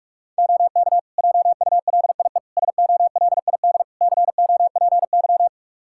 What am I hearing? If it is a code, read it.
Morse code, heard as a keyed tone, about 46 wpm, 700 Hz. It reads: OK JULIE SOLID COPY
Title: Practice QSO - Exchange 3 - More Info - 45wpm